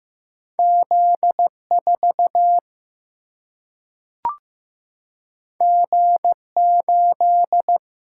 Morse code, heard as a keyed tone, about 15 wpm, 700 Hz. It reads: Z 4 G 8
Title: Single-Letter Single-Number 15wpm